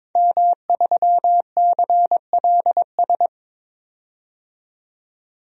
Morse code, heard as a keyed tone, about 22 wpm, 700 Hz. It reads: M3CLS
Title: DXCC-100 Call Signs - 5 Characters - 22wpm